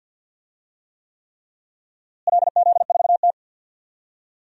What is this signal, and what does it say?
Morse code, heard as a keyed tone, about 52 wpm, 700 Hz. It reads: LZ4T
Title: DXCC-100 Call Signs - 4 Characters - Repeated 3x - 50wpm